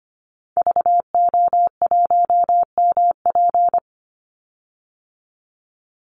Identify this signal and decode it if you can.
Morse code, heard as a keyed tone, about 25 wpm, 700 Hz. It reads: VO1MP
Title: Call Signs Repeated 2x - 25wpm